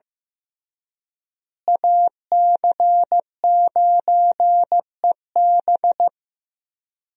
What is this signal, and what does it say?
Morse code, heard as a keyed tone, about 15 wpm, 700 Hz. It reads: AC9EB
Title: USA Call Signs - 5 Characters - Repeated 3x - 15wpm